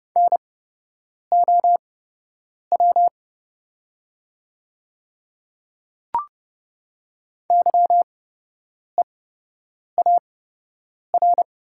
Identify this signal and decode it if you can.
Morse code, heard as a keyed tone, about 30 wpm, spaced as for 8 wpm, 700 Hz. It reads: NOW YEAR
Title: 8x Character Spacing - Top 200 Words - 30wpm